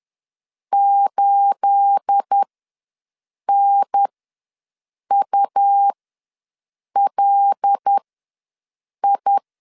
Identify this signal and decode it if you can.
Morse code, heard as a keyed tone, about 10 wpm, spaced as for 6 wpm, 800 Hz. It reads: 8NULI
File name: LEZIONI / 9 28 ws 10wpm 4124